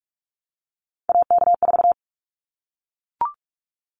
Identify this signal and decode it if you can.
Morse code, heard as a keyed tone, about 44 wpm, 700 Hz. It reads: AK4
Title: Callsign Prefixes for Top 25 DXCC - 45wpm